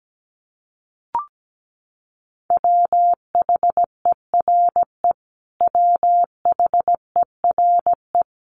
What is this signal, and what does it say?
Morse code, heard as a keyed tone, about 17 wpm, 700 Hz. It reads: WHERE WHERE
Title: Top 200 Words Repeated 17wpm